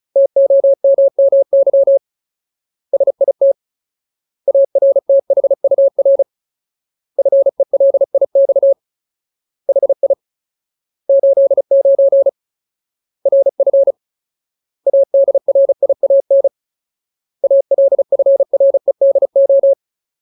Morse code, heard as a keyed tone, about 34 wpm, 550 Hz. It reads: TOMMY SIT ARTHUR FELIX HI 89 RF ADRIAN ALFREDO
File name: Ultimate Word List / 35WPM Ultimate